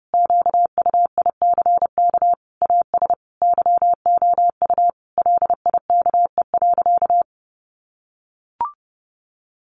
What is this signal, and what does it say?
Morse code, heard as a keyed tone, about 30 wpm, 700 Hz. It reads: QUICK AS YOU LIKE.
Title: Sentences from Top 500 Words 30wpm